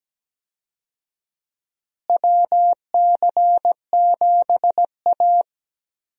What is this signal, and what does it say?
Morse code, heard as a keyed tone, about 17 wpm, 700 Hz. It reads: WC7A
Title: USA Call Signs - 4 Characters - Repeated 2x - 17wpm